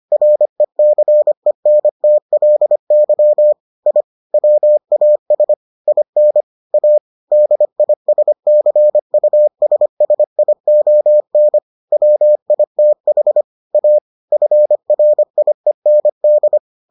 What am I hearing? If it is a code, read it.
Morse code, heard as a keyed tone, about 25 wpm, 600 Hz. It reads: RECENTLY I WAS IN A DISCUSSION WITH A FRIEND